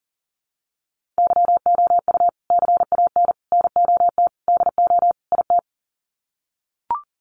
Morse code, heard as a keyed tone, about 40 wpm, 700 Hz. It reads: YOU CAN NOT DO IT
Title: Sets of 5 Words Encore2 40wpm